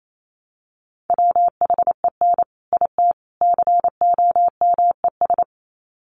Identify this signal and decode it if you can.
Morse code, heard as a keyed tone, about 28 wpm, 700 Hz. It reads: WHEN IT COMES
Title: Sets of 3 Words 28wpm